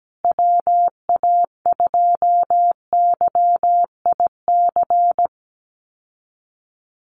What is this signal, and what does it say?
Morse code, heard as a keyed tone, about 17 wpm, 700 Hz. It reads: WA2YIC
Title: USA Call Signs Repeated 3x 17wpm